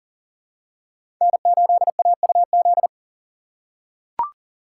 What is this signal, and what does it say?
Morse code, heard as a keyed tone, about 40 wpm, 700 Hz. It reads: N8AUZ
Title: USA Call Signs - 5 Characters - 3x Character Spacing - 40wpm